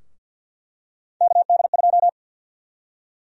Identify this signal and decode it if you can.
Morse code, heard as a keyed tone, about 49 wpm, 700 Hz. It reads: KD2
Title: USA Call Sign Prefixes - 50wpm